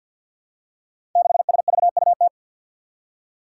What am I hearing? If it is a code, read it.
Morse code, heard as a keyed tone, about 51 wpm, 700 Hz. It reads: 6SVUT
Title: Five Challenging Letter-Number Combinations 50wpm